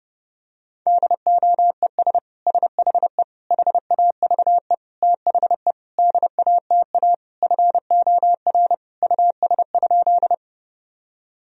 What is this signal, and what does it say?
Morse code, heard as a keyed tone, about 30 wpm, 700 Hz. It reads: DOES SHE HAVE THE DATA FOR US?
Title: Sentences from Top 1000 Words - Review Part 3 - 30wpm